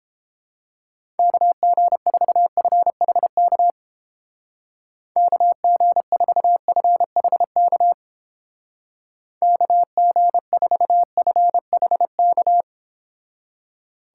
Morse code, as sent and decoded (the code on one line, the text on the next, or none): -.- --. ....- ..-. .... -.- / -.- --. ....- ..-. .... -.- / -.- --. ....- ..-. .... -.-
KG4FHK KG4FHK KG4FHK